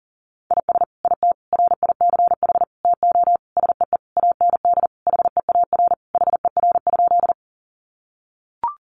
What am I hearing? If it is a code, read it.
Morse code, heard as a keyed tone, about 40 wpm, 700 Hz. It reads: IS IT RICH TO SEE AND HEAR HER?
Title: Sentences from Top 1100 Words 40wpm